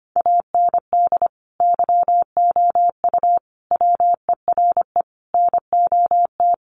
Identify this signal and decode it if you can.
Morse code, heard as a keyed tone, about 25 wpm, 700 Hz. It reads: AND YOU WERE NOT
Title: Sentences from Top 1400 Words 25wpm